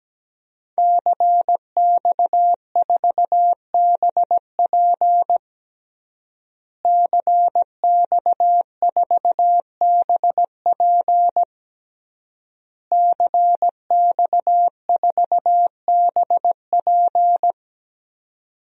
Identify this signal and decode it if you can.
Morse code, heard as a keyed tone, about 17 wpm, 700 Hz. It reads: CX4BP CX4BP CX4BP
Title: DXCC-100 Call Signs - 5 Characters - Repeated 3x - 17wpm